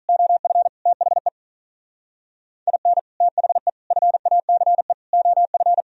Morse code, heard as a keyed tone, about 43 wpm, 700 Hz. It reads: OF THE IN THE FACE OF
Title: Speed-Racing - Sets of 5 Words 35wpm